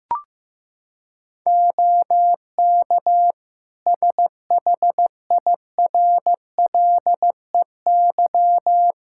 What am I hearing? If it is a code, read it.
Morse code, heard as a keyed tone, about 15 wpm, 700 Hz. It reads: OK SHIRLEY
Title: Practice QSO - Exchange 3 - More Info - 15wpm